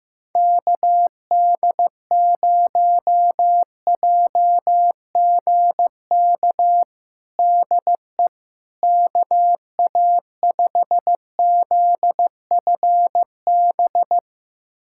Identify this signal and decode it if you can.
Morse code, heard as a keyed tone, about 15 wpm, 700 Hz. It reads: KD0JGK DE KA5ZFB
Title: USA Callsign DE Callsign - 6 Characters - 15wpm